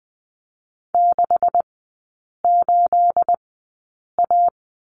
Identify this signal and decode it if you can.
Morse code, heard as a keyed tone, about 20 wpm, spaced as for 8 wpm, 700 Hz. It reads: 68A
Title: 212 Lesson 34e - Number 8 - Three Characters 20wpm